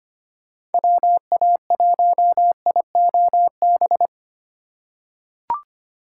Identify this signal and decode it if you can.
Morse code, heard as a keyed tone, about 25 wpm, 700 Hz. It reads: WA1IOB
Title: USA Call Signs - 6 Characters - 2x Character Spacing - 25wpm